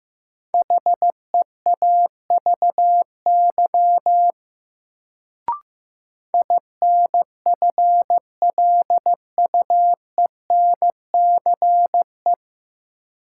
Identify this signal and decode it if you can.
Morse code, heard as a keyed tone, about 15 wpm, 700 Hz. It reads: HEAVY INFLUENCE
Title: Sentences from Top 1100 Words - New Word Drill Alt - 15wpm